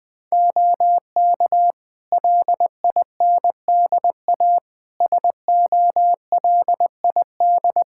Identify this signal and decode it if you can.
Morse code, heard as a keyed tone, about 20 wpm, 700 Hz. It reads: OK LINDA SOLID
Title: Practice QSO - Exchange 3 - More Info - 20wpm